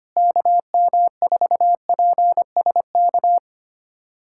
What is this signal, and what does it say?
Morse code, heard as a keyed tone, about 25 wpm, 700 Hz. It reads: KM4PSK
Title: USA Callsign DE Callsign 25wpm